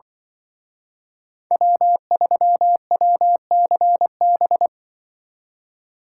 Morse code, heard as a keyed tone, about 25 wpm, 700 Hz. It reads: W3WCB